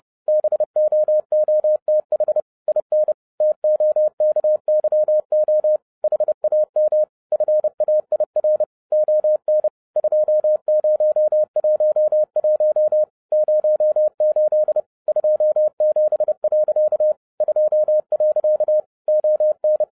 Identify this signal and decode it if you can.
Morse code, heard as a keyed tone, about 30 wpm, 610 Hz. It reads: BOOTH IN TOKYO HAM FAIR ON 2011 08 27. 2. ON